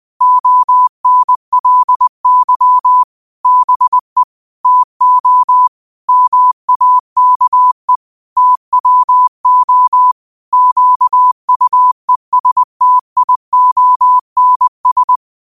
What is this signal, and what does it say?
Morse code, heard as a keyed tone, about 20 wpm, 1000 Hz. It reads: ONLY BE TO MAKE TWO QUESTIONS
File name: qotdfiles / 20wpm